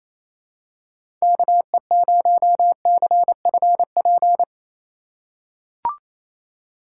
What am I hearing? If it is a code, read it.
Morse code, heard as a keyed tone, about 27 wpm, 700 Hz. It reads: KE0CFP
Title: USA Call Signs - 6 Characters - 4x Character Spacing - 28wpm